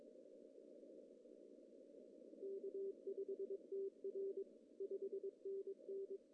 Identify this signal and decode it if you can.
Morse code, heard as a keyed tone, about 22 wpm, 390 Hz. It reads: K5TR 5NN
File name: CW / cqww-cw-k5tr